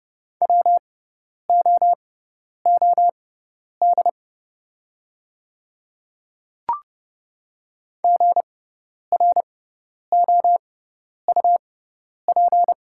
Morse code, heard as a keyed tone, about 30 wpm, spaced as for 10 wpm, 700 Hz. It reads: WOOD GROUP
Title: 6x Character Spacing - Top 300 Words - 30wpm